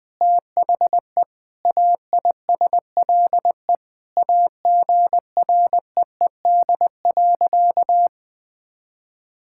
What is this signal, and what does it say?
Morse code, heard as a keyed tone, about 20 wpm, 700 Hz. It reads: THE AISLE AGREED.
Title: Sentences with 6-Word Phrases - 20wpm